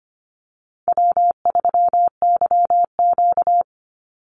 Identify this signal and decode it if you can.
Morse code, heard as a keyed tone, about 25 wpm, 700 Hz. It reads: W3YQ
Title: Call Signs Repeated 2x - 25wpm